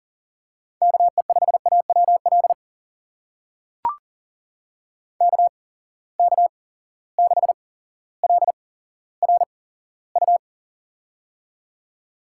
Morse code, heard as a keyed tone, about 40 wpm, spaced as for 17 wpm, 700 Hz. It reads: KE5AWL KK6LRU
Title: USA Call Signs - 8x Character Spacing - 40wpm